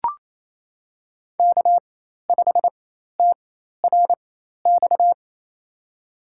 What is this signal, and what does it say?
Morse code, heard as a keyed tone, about 28 wpm, spaced as for 13 wpm, 700 Hz. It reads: K5TRX
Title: USA Call Signs - 5 Characters - 4x Character Spacing - 28wpm